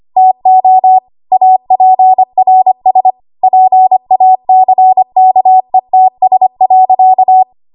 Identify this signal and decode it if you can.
Morse code, heard as a keyed tone, about 25 wpm, 750 Hz. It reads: TO APRS PACKETS.
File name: Morse / 25WPM